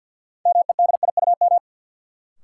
Morse code, heard as a keyed tone, about 52 wpm, 700 Hz. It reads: MEDIUM
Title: Sentences from Top 900 Words - New Word Drill - 50wpm